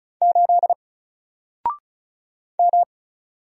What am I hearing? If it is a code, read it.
Morse code, heard as a keyed tone, about 36 wpm, 700 Hz. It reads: 8 M
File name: Single-Letter-Number-35wpm